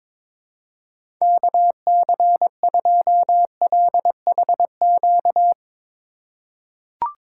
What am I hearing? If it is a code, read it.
Morse code, heard as a keyed tone, about 23 wpm, 700 Hz. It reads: KC2LHQ
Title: USA Call Signs - 6 Characters - 2x Character Spacing - 22wpm